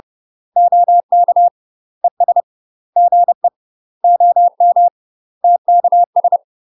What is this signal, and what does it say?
Morse code, heard as a keyed tone, about 30 wpm, 700 Hz. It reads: OK ES GE OM TKS
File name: UK MASTER QSOs TRAINING / 30WPM / UK MASTER QSOs 0042